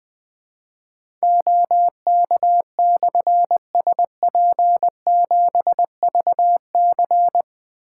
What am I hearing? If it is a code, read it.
Morse code, heard as a keyed tone, about 20 wpm, 700 Hz. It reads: OK/SP7VC
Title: Difficult Call Signs 20wpm